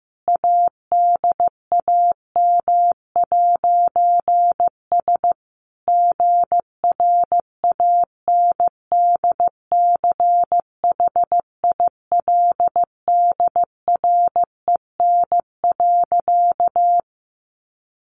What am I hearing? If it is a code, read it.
Morse code, heard as a keyed tone, about 15 wpm, 700 Hz. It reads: ADAM'S GRANDCHILDREN.
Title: A Sound Of Thunder 15wpm